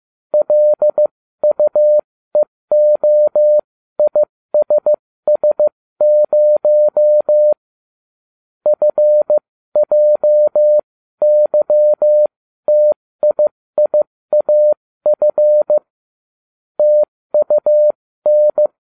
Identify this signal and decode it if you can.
Morse code, heard as a keyed tone, about 15 wpm, spaced as for 13 wpm, 600 Hz. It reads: LUEOISS0 FJYTIIAF TUN